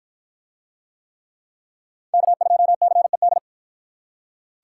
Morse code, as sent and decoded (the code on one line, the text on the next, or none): -.- ..--- -.-. -..
K2CED